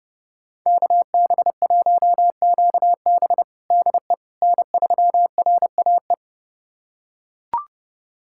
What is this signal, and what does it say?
Morse code, heard as a keyed tone, about 30 wpm, 700 Hz. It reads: KB1QB DE N3RAE
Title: USA Callsign DE Callsign - 5 Characters - 30wpm